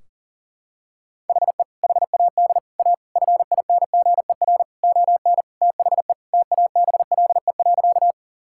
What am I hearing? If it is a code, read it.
Morse code, heard as a keyed tone, about 37 wpm, 700 Hz. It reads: HE HAD A FINGER ON THE TABLE.
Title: Sentences from Top 1600 Words 40wpm